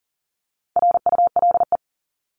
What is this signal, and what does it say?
Morse code, heard as a keyed tone, about 40 wpm, 700 Hz. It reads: RULE
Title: ES:S Suffix - Top 100 Words with Base Words 40wpm